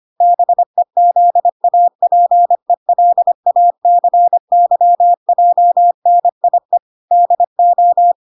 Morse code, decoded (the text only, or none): BEZAPELACYJNIE DO